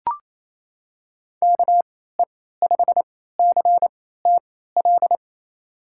Morse code, heard as a keyed tone, about 28 wpm, spaced as for 16 wpm, 700 Hz. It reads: KE5CTL
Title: USA Call Signs - 6 Characters - 3x Character Spacing - 28wpm